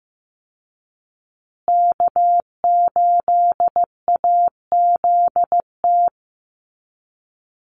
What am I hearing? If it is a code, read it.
Morse code, heard as a keyed tone, about 15 wpm, 700 Hz. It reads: K8AZT
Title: Call Signs Repeated 3x - 15wpm